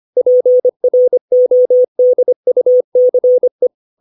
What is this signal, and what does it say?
Morse code, heard as a keyed tone, about 25 wpm, 500 Hz. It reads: PRODUCE
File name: 632 words / 632 Words 25WPM